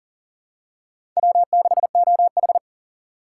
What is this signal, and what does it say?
Morse code, heard as a keyed tone, about 40 wpm, 700 Hz. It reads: W6OH
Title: USA Call Signs - 4 Characters - 6x Character Spacing - 40wpm